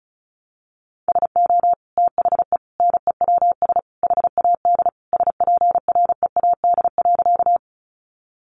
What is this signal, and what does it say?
Morse code, heard as a keyed tone, about 34 wpm, 700 Hz. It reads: SO THE NEWS HAD SPREAD.
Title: Sentences from Top 1200 Words 35wpm